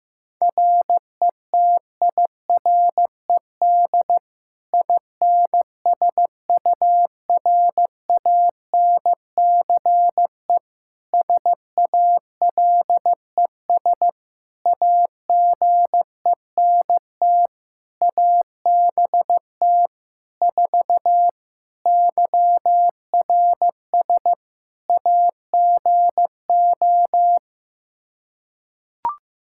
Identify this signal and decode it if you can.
Morse code, heard as a keyed tone, about 15 wpm, 700 Hz. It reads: RETIRED INSURANCE SALES AGENT ABT 4 YRS AGO
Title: K1USN SST QSO Practice 15wpm